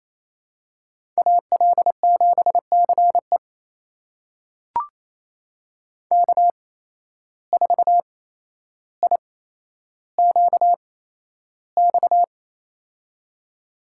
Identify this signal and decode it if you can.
Morse code, heard as a keyed tone, about 28 wpm, spaced as for 12 wpm, 700 Hz. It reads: AL7CE K4IQX
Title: USA Call Signs - 5 Characters - 8x Character Spacing - 28wpm